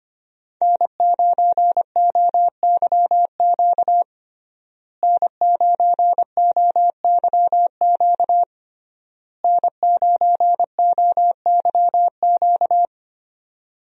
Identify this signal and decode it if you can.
Morse code, heard as a keyed tone, about 25 wpm, 700 Hz. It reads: N9OYQ N9OYQ N9OYQ
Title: USA Call Signs - 5 Characters - Repeated 3x - 25wpm